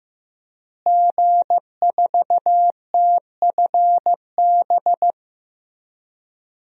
Morse code, heard as a keyed tone, about 15 wpm, 700 Hz. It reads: G4TFB
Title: DXCC-100 Call Signs - 5 Characters - Repeated 3x - 15wpm